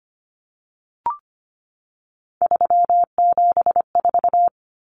Morse code, heard as a keyed tone, about 25 wpm, 700 Hz. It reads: 374
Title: Numbers - 3-Digit 25wpm